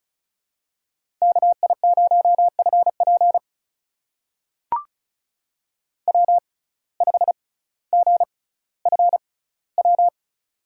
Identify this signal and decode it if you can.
Morse code, heard as a keyed tone, about 36 wpm, spaced as for 18 wpm, 700 Hz. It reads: KI0FP W5GFW
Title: USA Call Signs - 5 Characters - 6x Character Spacing - 35wpm